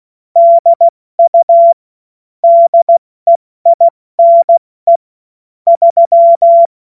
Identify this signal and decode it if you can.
Morse code, heard as a keyed tone, about 16 wpm, 680 Hz. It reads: DU DEINE 3